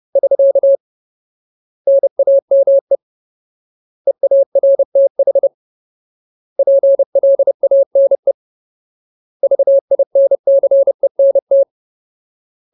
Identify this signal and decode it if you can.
Morse code, heard as a keyed tone, about 30 wpm, 550 Hz. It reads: <SK> NAME EARTH PLANE VINCENT
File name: Ultimate Word List / 30WPM Ultimate